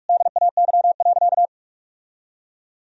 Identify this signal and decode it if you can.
Morse code, heard as a keyed tone, about 46 wpm, 700 Hz. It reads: DAY.
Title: Sentences from Top 1000 Words - Review Part 1 - 45wpm